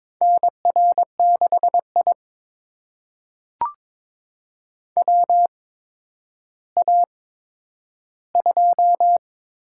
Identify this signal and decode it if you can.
Morse code, heard as a keyed tone, about 22 wpm, spaced as for 11 wpm, 700 Hz. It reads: NR6I WA2
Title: USA Call Signs - 4 Characters - 8x Character Spacing - 22wpm